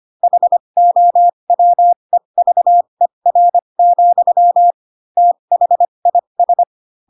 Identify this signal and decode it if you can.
Morse code, heard as a keyed tone, about 25 wpm, spaced as for 22 wpm, 700 Hz. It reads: HOWEVER, THIS
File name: Pinocchio / Pinocchio ch4 18 25 ik1hge